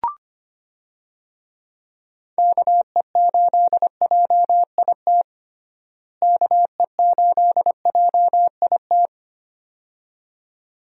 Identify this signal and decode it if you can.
Morse code, heard as a keyed tone, about 25 wpm, 700 Hz. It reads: KE8JIT KE8JIT
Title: USA Call Signs - 6 Characters - Repeated 2x - 25wpm